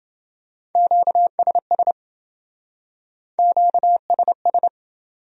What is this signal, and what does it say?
Morse code, heard as a keyed tone, about 27 wpm, 700 Hz. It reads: QSS QSS